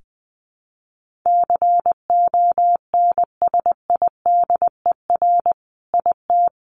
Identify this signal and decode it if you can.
Morse code, heard as a keyed tone, about 20 wpm, 700 Hz. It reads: CONSIDER IT
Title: Sentences from Top 1400 Words 20wpm